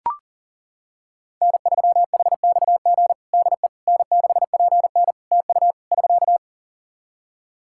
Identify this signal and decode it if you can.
Morse code, heard as a keyed tone, about 37 wpm, 700 Hz. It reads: N3HXG DE N6PN TU <SK>
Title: K1USN SST QSO Practice 40wpm